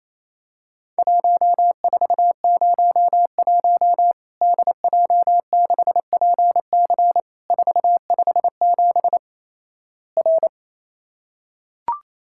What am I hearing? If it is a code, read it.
Morse code, heard as a keyed tone, about 28 wpm, 700 Hz. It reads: 1401 DJ6PC 457 R
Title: Worked All Europe DX Contest - QTC Exchange - 28wpm